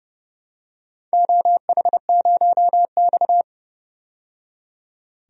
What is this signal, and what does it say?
Morse code, heard as a keyed tone, about 31 wpm, 700 Hz. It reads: OH0X
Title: DXCC-100 Call Signs - 4 Characters - 30wpm